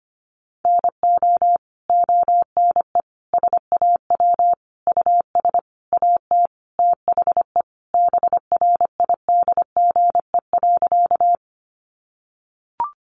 Morse code, heard as a keyed tone, about 25 wpm, 700 Hz. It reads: NO ONE SAW US AT THE BRIDGE.
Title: Sentences from Top 1200 Words 25wpm